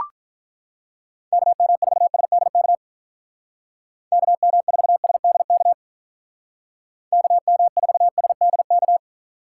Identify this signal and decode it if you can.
Morse code, heard as a keyed tone, about 43 wpm, 700 Hz. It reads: KM4SDK KM4SDK KM4SDK
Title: Speed Racing - USA 2x3 Call Signs 35wpm